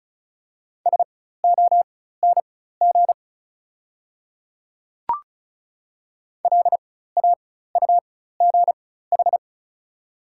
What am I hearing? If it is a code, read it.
Morse code, heard as a keyed tone, about 34 wpm, spaced as for 16 wpm, 690 Hz. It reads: SONG LAUGH
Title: 4x Character Spacing - Top 500 Words - 35wpm